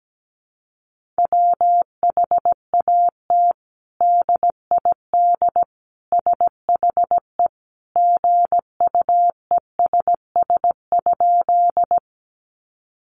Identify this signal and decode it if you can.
Morse code, heard as a keyed tone, about 17 wpm, 700 Hz. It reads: WHAT DID SHE GUESS?
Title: Sentences from Top 1500 Words 17wpm